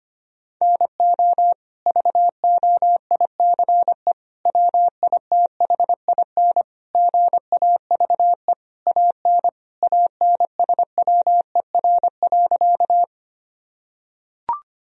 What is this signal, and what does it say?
Morse code, heard as a keyed tone, about 25 wpm, 700 Hz. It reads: NO VOICE WITHIN GAVE AN ANSWER.
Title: Sentences from Top 1000 Words - Review Part 4 - 25wpm